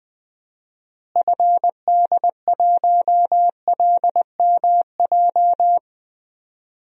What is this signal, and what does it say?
Morse code, heard as a keyed tone, about 20 wpm, 700 Hz. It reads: FD1LMJ
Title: DXCC-100 Call Signs - 6 Characters - Repeated 2x - 20wpm